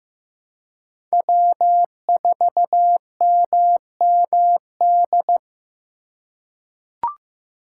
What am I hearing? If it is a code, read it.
Morse code, heard as a keyed tone, about 15 wpm, 700 Hz. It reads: W4MMD
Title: K1USN SST QSO Practice - 15wpm